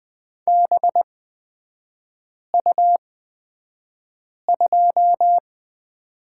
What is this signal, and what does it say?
Morse code, heard as a keyed tone, about 20 wpm, spaced as for 5 wpm, 700 Hz. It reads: BU2
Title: 224 Lesson 36e - BT - Three Characters 20wpm